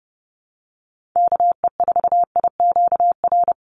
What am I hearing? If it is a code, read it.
Morse code, heard as a keyed tone, about 30 wpm, 700 Hz. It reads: KE4IQR
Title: USA Call Signs 30wpm